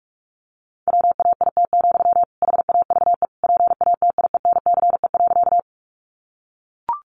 Morse code, heard as a keyed tone, about 44 wpm, 700 Hz. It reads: WAIT, HAVE PATIENCE.
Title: Sentences from Top 1300 Words 45wpm